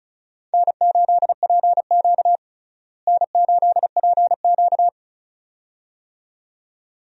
Morse code, heard as a keyed tone, about 34 wpm, 700 Hz. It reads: N8PQ N8PQ
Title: USA Call Signs - 4 Characters - Repeated 2x - 35wpm